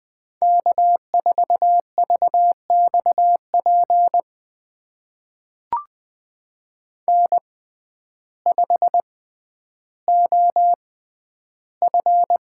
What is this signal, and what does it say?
Morse code, heard as a keyed tone, about 20 wpm, spaced as for 11 wpm, 700 Hz. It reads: K4VXP N5OF
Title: USA Call Signs - 5 Characters - 6x Character Spacing - 20wpm